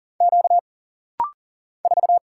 Morse code, heard as a keyed tone, about 40 wpm, 700 Hz. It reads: Q 4
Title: Single Letter-Number - Rapid-Fire 40wpm